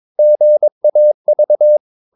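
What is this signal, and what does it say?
Morse code, heard as a keyed tone, about 22 wpm, 600 Hz. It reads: GAV
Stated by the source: Morse practice of short Danish words at 22 wpm, from CW med Gnister.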